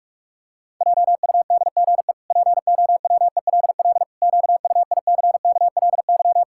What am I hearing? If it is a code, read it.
Morse code, heard as a keyed tone, about 46 wpm, 700 Hz. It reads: JUDGE POWELL QUICKLY
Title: Pangram Challenge v2 - 45wpm